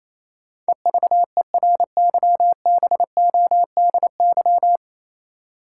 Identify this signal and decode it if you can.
Morse code, heard as a keyed tone, about 28 wpm, 700 Hz. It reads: EVERYBODY
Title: Sentences from Top 1400 Words - New Word Drill - 28wpm